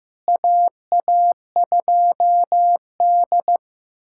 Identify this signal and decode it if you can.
Morse code, heard as a keyed tone, about 15 wpm, 700 Hz. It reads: AA2D
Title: USA Callsign DE Callsign - 4 Characters - 15wpm